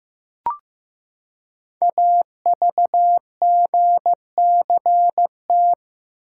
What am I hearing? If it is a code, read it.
Morse code, heard as a keyed tone, about 16 wpm, 700 Hz. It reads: AVGCT
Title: Code Groups - 5 Letter:Numbers - 15wpm